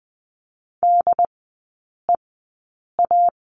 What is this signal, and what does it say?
Morse code, heard as a keyed tone, about 20 wpm, spaced as for 8 wpm, 700 Hz. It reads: DEA
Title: 191 Lesson 31e - Letter v - Three Characters 20wpm